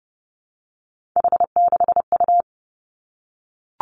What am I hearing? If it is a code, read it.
Morse code, heard as a keyed tone, about 30 wpm, 700 Hz. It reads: H6U
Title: Three Challenging Letter-Number Combinations 30wpm